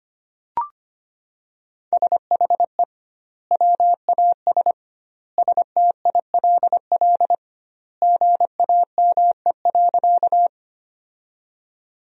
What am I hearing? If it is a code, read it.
Morse code, heard as a keyed tone, about 25 wpm, 700 Hz. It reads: SHE WAS STILL GAME.
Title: 2x Word-Spacing - Sentences from Top 500 Words - 25wpm